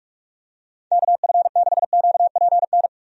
Morse code, heard as a keyed tone, about 42 wpm, 700 Hz. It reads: KF6QPN
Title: USA Call Signs - 6 Characters - Repeated 2x - 45wpm